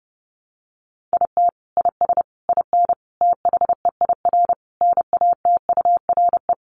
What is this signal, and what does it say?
Morse code, heard as a keyed tone, about 30 wpm, 700 Hz. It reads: IT IS IN THEIR NATURE
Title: Sentences from Top 1300 Words 30wpm